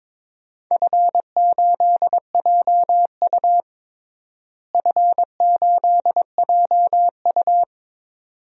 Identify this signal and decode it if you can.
Morse code, heard as a keyed tone, about 22 wpm, 700 Hz. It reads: F8JU F8JU
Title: DXCC-100 Call Signs - 4 Characters - Repeated 2x - 22wpm